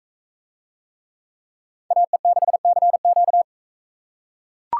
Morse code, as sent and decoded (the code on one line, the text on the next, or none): .- . -.... -.-. --.-
AE6CQ